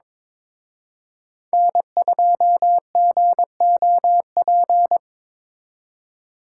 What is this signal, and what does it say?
Morse code, heard as a keyed tone, about 25 wpm, 700 Hz. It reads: N2GOP
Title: Speed Racing - USA Call Signs 22wpm